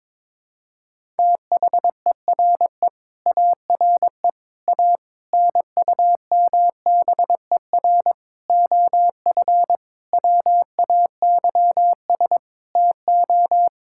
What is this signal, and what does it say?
Morse code, heard as a keyed tone, about 21 wpm, 700 Hz. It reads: THERE ARE A NUMBER OF WAYS TO
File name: Sentences with 7-Word Phrases - 22wpm